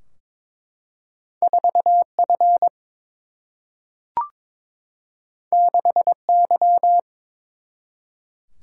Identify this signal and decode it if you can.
Morse code, heard as a keyed tone, about 22 wpm, 700 Hz. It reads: 4 F 6 Y
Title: Single-Letter Single-Number 22wpm